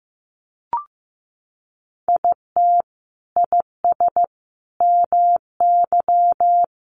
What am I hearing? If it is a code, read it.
Morse code, heard as a keyed tone, about 15 wpm, 700 Hz. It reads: IT IS MY
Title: Sentences from Top 1500 Words 15wpm